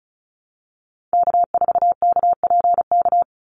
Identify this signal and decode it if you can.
Morse code, heard as a keyed tone, about 34 wpm, 700 Hz. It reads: K4KPK
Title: NA SOTA Activator Callsigns 35wpm